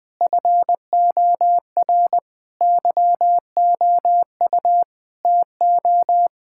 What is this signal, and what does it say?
Morse code, heard as a keyed tone, about 20 wpm, 700 Hz. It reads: FOR YOU TO
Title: Sentences from Top 1000 Words - Review Part 6 - 20wpm